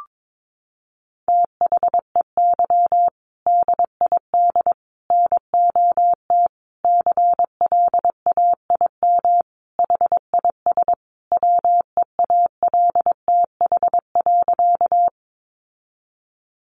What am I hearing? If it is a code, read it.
Morse code, heard as a keyed tone, about 22 wpm, 700 Hz. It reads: THEY DID NOT CLAIM HIS WEALTH.
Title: Sentences from Top 1500 Words 22wpm